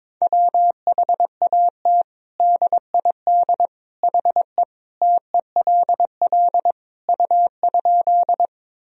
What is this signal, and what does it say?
Morse code, heard as a keyed tone, about 22 wpm, 700 Hz. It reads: WHAT DID HE TELL U?
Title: Sentences from Top 200 Words and Abbreviations 22wpm